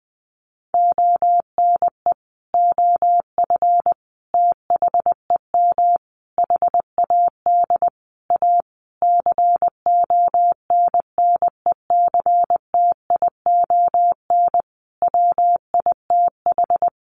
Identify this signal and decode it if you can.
Morse code, heard as a keyed tone, about 20 wpm, 700 Hz. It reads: ONE OF THEM HAD A CONNECTION WITH
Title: Sentences from Top 1300 Words 20wpm